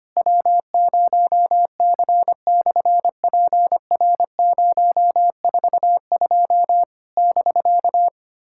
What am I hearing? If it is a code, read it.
Morse code, heard as a keyed tone, about 24 wpm, 700 Hz. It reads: W0C/PR042 <BK>
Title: SOTA QSO Practice 25wpm